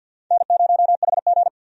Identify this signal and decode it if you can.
Morse code, heard as a keyed tone, about 48 wpm, 700 Hz. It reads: N0HG
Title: USA Call Signs - 4 Characters - 50wpm